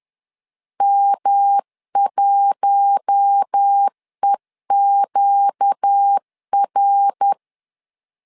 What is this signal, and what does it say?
Morse code, heard as a keyed tone, about 10 wpm, 800 Hz. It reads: M1EQR
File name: LEZIONI / 4 Q1 10wpm 2448